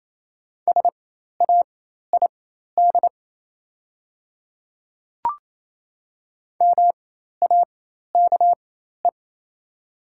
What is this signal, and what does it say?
Morse code, heard as a keyed tone, about 28 wpm, spaced as for 13 wpm, 700 Hz. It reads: SAID MAKE
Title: 4x Character Spacing - Top 200 Words - 28wpm